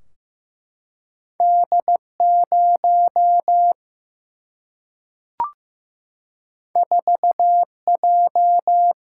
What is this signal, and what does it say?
Morse code, heard as a keyed tone, about 15 wpm, 700 Hz. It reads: D0 4J
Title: Single-Letter Single-Number 15wpm